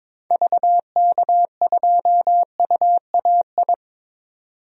Morse code, heard as a keyed tone, about 22 wpm, 700 Hz. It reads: VK2UAI
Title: DXCC-100 Call Signs - 6 Characters - 22wpm